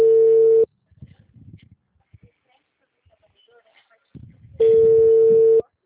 Ringback tone (China, 450 Hz).